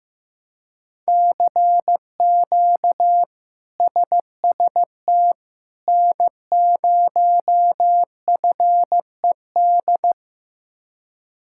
Morse code, heard as a keyed tone, about 15 wpm, 700 Hz. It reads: CQ SST N0FED
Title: K1USN SST QSO Practice - 15wpm